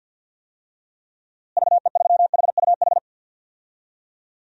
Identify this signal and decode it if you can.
Morse code, heard as a keyed tone, about 46 wpm, 700 Hz. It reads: VE3HUH
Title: DXCC-100 Call Signs - 6 Characters - Repeated 3x - 50wpm